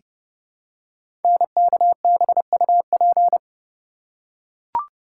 Morse code, heard as a keyed tone, about 30 wpm, 700 Hz. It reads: NKBUP
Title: Code Groups - 5 Letters - 30wpm